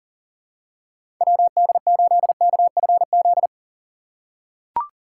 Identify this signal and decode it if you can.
Morse code, heard as a keyed tone, about 40 wpm, 700 Hz. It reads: WD8KFZ
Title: USA Call Signs - 6 Characters - 3x Character Spacing - 40wpm